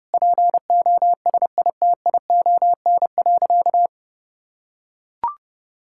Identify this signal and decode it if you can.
Morse code, heard as a keyed tone, about 30 wpm, 700 Hz. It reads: POSITION.
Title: Sentences from Top 1000 Words - Review Part 5 - 30wpm